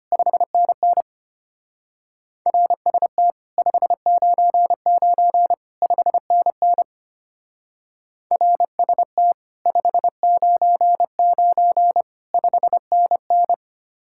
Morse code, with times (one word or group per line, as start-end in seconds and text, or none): 0.1-1.0 5NN
2.5-3.3 RST
3.6-5.5 599
5.8-6.8 5NN
8.3-9.3 RST
9.7-12.0 599
12.3-13.5 5NN